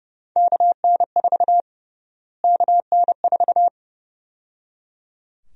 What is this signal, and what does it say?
Morse code, heard as a keyed tone, about 30 wpm, 700 Hz. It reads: KN4 KN4
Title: USA Call Sign Prefixes - Repeated 2x - 30wpm